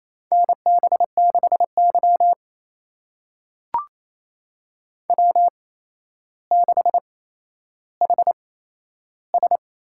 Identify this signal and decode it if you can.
Morse code, heard as a keyed tone, about 27 wpm, spaced as for 12 wpm, 700 Hz. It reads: NB6Y W6HS
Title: USA Call Signs - 8x Character Spacing - 28wpm